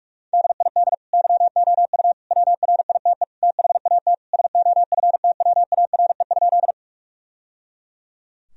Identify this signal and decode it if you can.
Morse code, heard as a keyed tone, about 46 wpm, 700 Hz. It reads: DID YOU WRITE THAT SOFTWARE?
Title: Sentences from Top 1000 Words - Review Part 3 - 45wpm